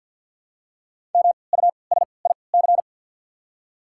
Morse code, heard as a keyed tone, about 53 wpm, spaced as for 29 wpm, 690 Hz. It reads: MUSIC
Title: 3x Character Spacing - Top 300 Words - 50wpm